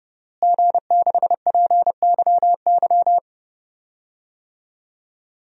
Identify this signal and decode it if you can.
Morse code, heard as a keyed tone, about 30 wpm, 700 Hz. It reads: G6PYY
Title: DXCC-100 Call Signs - 5 Characters - Repeated 3x - 30wpm